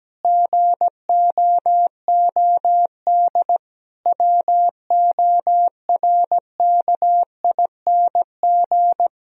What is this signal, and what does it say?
Morse code, heard as a keyed tone, about 17 wpm, 700 Hz. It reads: GOOD WORKING
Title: Sentences from Top 1000 Words - Review Part 4 - 17wpm